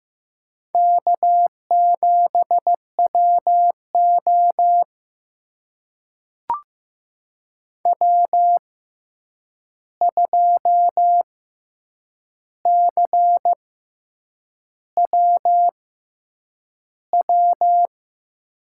Morse code, heard as a keyed tone, about 15 wpm, spaced as for 7 wpm, 700 Hz. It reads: K7WO W2CWW
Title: USA Call Signs - 6x Character Spacing - 15wpm